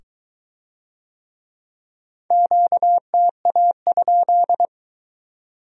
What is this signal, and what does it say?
Morse code, heard as a keyed tone, about 22 wpm, 700 Hz. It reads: QTA?